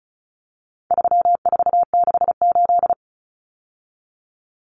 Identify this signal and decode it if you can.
Morse code, heard as a keyed tone, about 34 wpm, 700 Hz. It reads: 3468
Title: Numbers - 4-Digit 35wpm